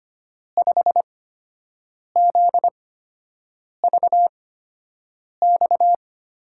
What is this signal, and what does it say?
Morse code, heard as a keyed tone, about 25 wpm, spaced as for 7 wpm, 700 Hz. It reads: 5ZVX
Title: USA Call Signs - 8x Character Spacing - 25wpm